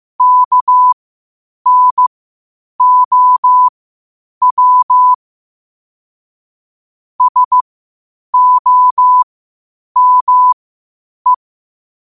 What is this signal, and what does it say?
Morse code, heard as a keyed tone, about 15 wpm, spaced as for 8 wpm, 1000 Hz. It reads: KNOW SOME
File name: qotdfiles / 8wpm